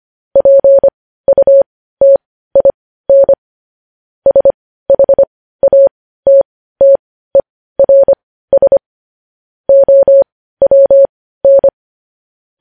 Morse code, heard as a keyed tone, about 25 wpm, spaced as for 15 wpm, 550 Hz. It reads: PUTIN SHATTERS OWN
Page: news-headlines-in-morse-code-at-15-wpm